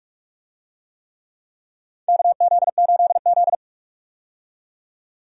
Morse code, heard as a keyed tone, about 42 wpm, 700 Hz. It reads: KZ8Z
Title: USA Call Signs - 4 Characters - 45wpm